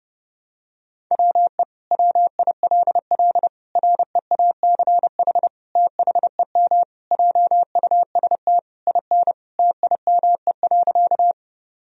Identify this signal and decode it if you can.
Morse code, heard as a keyed tone, about 30 wpm, 700 Hz. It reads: WE WILL REACH THEM JUST IN TIME.
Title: Sentences from Top 1000 Words - Review Part 2 - 30wpm